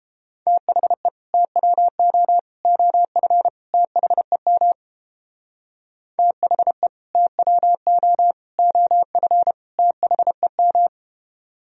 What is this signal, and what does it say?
Morse code, heard as a keyed tone, about 31 wpm, 700 Hz. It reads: THE TWO OF THEM THE TWO OF THEM